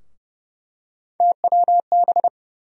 Morse code, heard as a keyed tone, about 31 wpm, 700 Hz. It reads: TWB
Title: USA Call Sign Postfixes - 3 Letters 30wpm